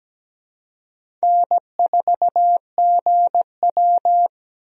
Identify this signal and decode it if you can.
Morse code, heard as a keyed tone, about 17 wpm, 700 Hz. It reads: N4GW
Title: USA Call Signs - 4 Characters - Repeated 2x - 17wpm